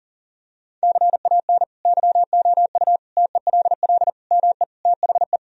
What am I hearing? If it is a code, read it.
Morse code, heard as a keyed tone, about 40 wpm, 700 Hz. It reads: CAN YOU TELL ME THE
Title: Sentences from Top 1000 Words - Review Part 1 - 40wpm